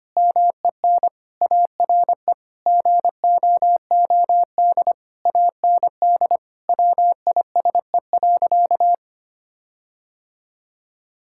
Morse code, heard as a keyed tone, about 25 wpm, 700 Hz. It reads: MEN ARE GOOD AND WISE.
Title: Sentences from Top 1600 Words 25wpm